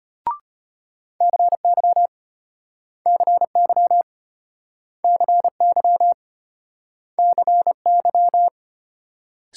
Morse code, heard as a keyed tone, about 33 wpm, 700 Hz. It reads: CY CY CY CY